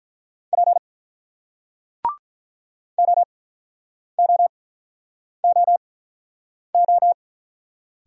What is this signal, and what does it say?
Morse code, heard as a keyed tone, about 47 wpm, spaced as for 9 wpm, 690 Hz. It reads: P OOOO